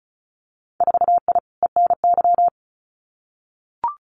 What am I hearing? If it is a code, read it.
Morse code, heard as a keyed tone, about 34 wpm, 700 Hz. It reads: 4I ENY
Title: Winter Field Day Exchanges 35wpm